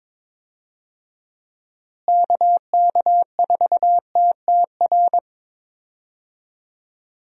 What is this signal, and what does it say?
Morse code, heard as a keyed tone, about 22 wpm, 700 Hz. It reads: KK4TTR
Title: USA Call Signs - 6 Characters - Repeated 2x - 22wpm